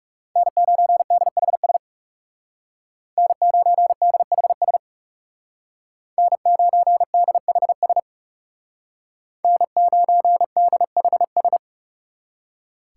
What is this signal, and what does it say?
Morse code, heard as a keyed tone, about 35 wpm, 700 Hz. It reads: N9DHS N9DHS N9DHS N9DHS
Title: Speed Racing - USA Call Signs 30wpm